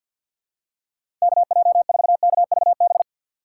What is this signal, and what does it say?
Morse code, heard as a keyed tone, about 47 wpm, 700 Hz. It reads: KJ4KVB
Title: USA Call Signs - 6 Characters - 6x Character Spacing - 50wpm